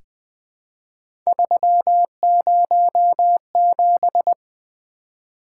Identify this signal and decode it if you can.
Morse code, heard as a keyed tone, about 20 wpm, 700 Hz. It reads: 307
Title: Three Letter-Number Combinations - 20wpm